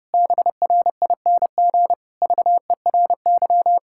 Morse code, heard as a keyed tone, about 30 wpm, 700 Hz. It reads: BRING VERY
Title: Pangram Challenge v2 - 30wpm